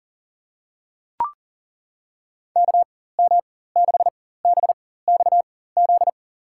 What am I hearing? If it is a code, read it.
Morse code, heard as a keyed tone, about 37 wpm, spaced as for 18 wpm, 700 Hz. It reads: KM6BXZ
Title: USA Call Signs - 4x Character Spacing - 40wpm